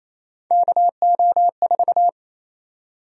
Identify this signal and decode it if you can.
Morse code, heard as a keyed tone, about 27 wpm, 700 Hz. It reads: KO4
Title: USA Call Sign Prefixes - Repeated 2x - 28wpm